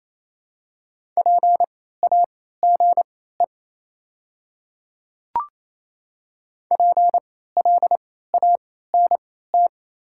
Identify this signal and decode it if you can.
Morse code, heard as a keyed tone, about 28 wpm, spaced as for 16 wpm, 700 Hz. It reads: PAGE PLANT
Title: 3x Character Spacing - Top 500 Words - 28wpm